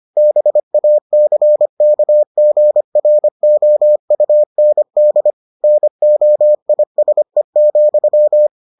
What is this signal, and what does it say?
Morse code, heard as a keyed tone, about 25 wpm, 600 Hz. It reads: BACKGROUND NOISE,